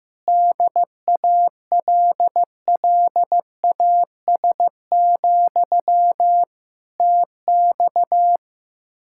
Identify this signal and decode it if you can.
Morse code, heard as a keyed tone, about 15 wpm, 700 Hz. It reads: DALLAS, TX
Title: Speed-Racing Common QSO Phrases Encore - 15wpm